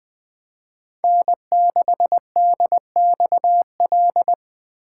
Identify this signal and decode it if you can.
Morse code, heard as a keyed tone, about 21 wpm, 700 Hz. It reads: N6DXL
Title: USA Call Signs - 5 Characters - 4x Character Spacing - 20wpm